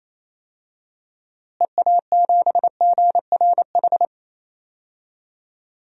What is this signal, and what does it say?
Morse code, heard as a keyed tone, about 28 wpm, 700 Hz. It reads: EA7GRH
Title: DXCC-100 Call Signs - 6 Characters - Repeated 2x - 28wpm